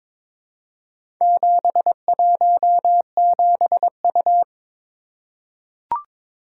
Morse code, heard as a keyed tone, about 21 wpm, 690 Hz. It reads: 717U
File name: practice / 22wpm / Four Letter-Number Combinations - 22wpm